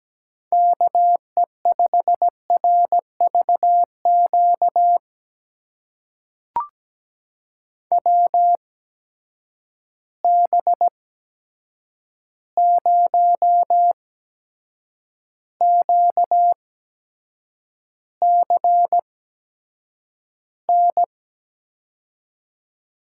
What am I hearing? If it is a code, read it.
Morse code, heard as a keyed tone, about 17 wpm, spaced as for 7 wpm, 700 Hz. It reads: KE5RVQ WB0QCN
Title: USA Call Signs - 6 Characters - 8x Character Spacing - 17wpm